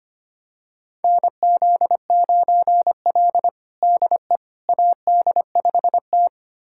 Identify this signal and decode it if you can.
Morse code, heard as a keyed tone, about 25 wpm, 700 Hz. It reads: NZ9L DE AD5T
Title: USA Callsign DE Callsign - 4 Characters - 25wpm